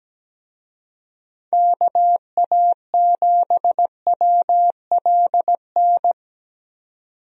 Morse code, heard as a keyed tone, about 17 wpm, 700 Hz. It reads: KA7WLN
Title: USA Call Signs - 6 Characters - 17wpm